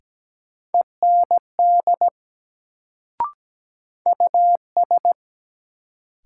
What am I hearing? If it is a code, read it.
Morse code, heard as a keyed tone, about 17 wpm, 700 Hz. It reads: END US
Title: Sentences from Top 200 Words - New Word Drill - 17wpm